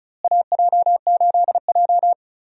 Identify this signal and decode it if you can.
Morse code, heard as a keyed tone, about 34 wpm, 700 Hz. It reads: AJ8J
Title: USA Call Signs - 4 Characters - Repeated 3x - 35wpm